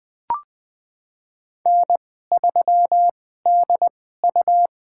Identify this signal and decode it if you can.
Morse code, heard as a keyed tone, about 20 wpm, spaced as for 14 wpm, 700 Hz. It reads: N3DU
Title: USA Call Signs - 4 Characters - 2x Character Spacing - 20wpm